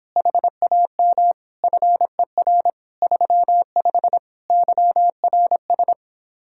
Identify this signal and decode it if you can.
Morse code, heard as a keyed tone, about 29 wpm, 700 Hz. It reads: HAM FER 35 YRS